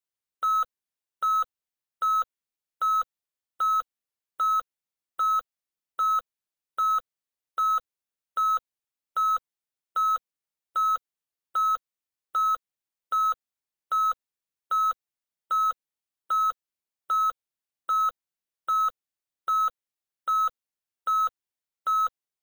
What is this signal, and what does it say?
There is no code.